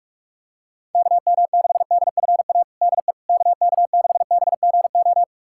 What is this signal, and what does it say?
Morse code, heard as a keyed tone, about 46 wpm, 700 Hz. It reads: KM6DFA DE KK6BGO
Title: USA Callsign DE Callsign - 6 Characters - 45wpm